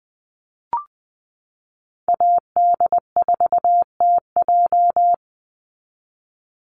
Morse code, heard as a keyed tone, about 20 wpm, 700 Hz. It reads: AD4TJ
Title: Call Signs - 20wpm